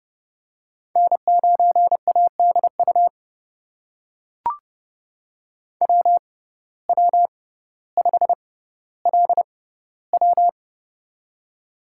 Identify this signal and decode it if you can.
Morse code, heard as a keyed tone, about 31 wpm, spaced as for 16 wpm, 700 Hz. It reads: N9ADU WW5LW
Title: USA Call Signs - 5 Characters - 6x Character Spacing - 30wpm